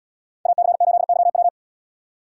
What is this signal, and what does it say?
Morse code, heard as a keyed tone, about 76 wpm, 700 Hz. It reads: AC8YW